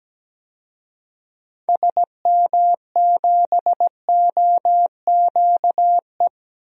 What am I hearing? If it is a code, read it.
Morse code, heard as a keyed tone, about 17 wpm, 700 Hz. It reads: SM7OQE
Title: DXCC-100 Call Signs - 6 Characters - 17wpm